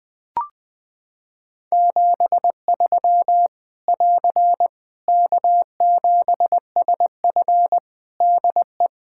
Morse code, heard as a keyed tone, about 20 wpm, 700 Hz. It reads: 73 <AR> K7SF DE
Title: Practice QSO - Exchange 4 - Ending - 20wpm